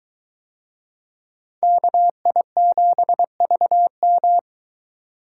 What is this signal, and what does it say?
Morse code, heard as a keyed tone, about 23 wpm, 700 Hz. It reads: KI7VM